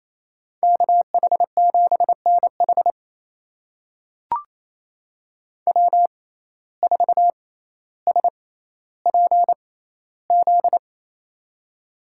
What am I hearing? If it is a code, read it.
Morse code, heard as a keyed tone, about 28 wpm, spaced as for 14 wpm, 700 Hz. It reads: KH7NH W4SPZ
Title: USA Call Signs - 6x Character Spacing - 28wpm